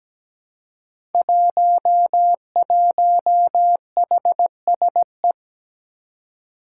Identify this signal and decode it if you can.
Morse code, heard as a keyed tone, about 17 wpm, 700 Hz. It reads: 11HSE
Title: Five Challenging Letter-Number Combinations 17wpm